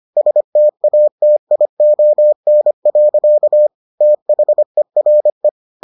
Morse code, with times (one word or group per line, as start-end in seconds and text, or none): 0.2-3.7 STATION.
4.0-5.5 THERE